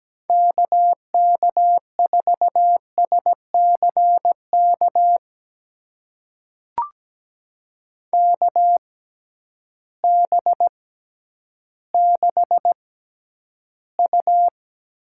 Morse code, heard as a keyed tone, about 17 wpm, spaced as for 10 wpm, 700 Hz. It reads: KK4SCK KB6U
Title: USA Call Signs - 6 Characters - 6x Character Spacing - 17wpm